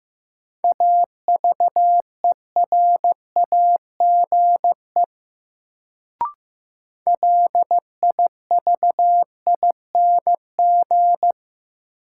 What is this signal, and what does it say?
Morse code, heard as a keyed tone, about 15 wpm, 700 Hz. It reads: AVERAGE LIVING
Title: Sentences from Top 1000 Words - Review Part 6 - New Word Drill - 15wpm